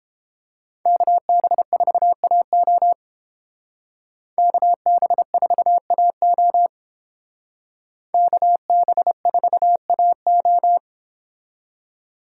Morse code, heard as a keyed tone, about 29 wpm, 700 Hz. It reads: KB4AO KB4AO KB4AO